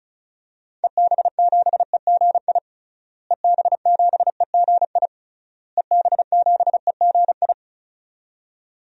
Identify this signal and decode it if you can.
Morse code, heard as a keyed tone, about 36 wpm, 700 Hz. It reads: EB7EGI EB7EGI EB7EGI
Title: DXCC-100 Call Signs - 6 Characters - Repeated 3x - 35wpm